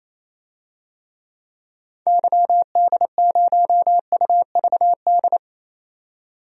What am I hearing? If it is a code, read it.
Morse code, heard as a keyed tone, about 28 wpm, 700 Hz. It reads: YD0UVD
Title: DXCC-100 Call Signs - 6 Characters - 28wpm